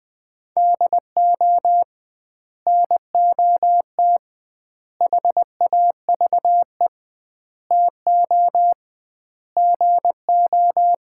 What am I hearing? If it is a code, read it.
Morse code, heard as a keyed tone, about 21 wpm, 700 Hz. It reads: DO NOT HAVE TO GO
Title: 2x Word-Spacing - Sets of 5 Words - 20wpm